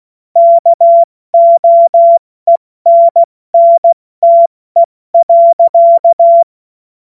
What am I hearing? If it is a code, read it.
Morse code, heard as a keyed tone, about 16 wpm, 680 Hz. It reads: KOENNTE.